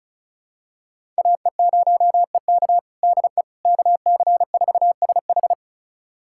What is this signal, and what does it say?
Morse code, heard as a keyed tone, about 36 wpm, 700 Hz. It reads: AE0EK DE KC4SH
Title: USA Callsign DE Callsign - 5 Characters - 35wpm